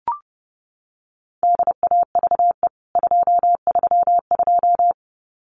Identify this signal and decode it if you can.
Morse code, heard as a keyed tone, about 30 wpm, 700 Hz. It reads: DAVE 232
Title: CWT Exchange 30wpm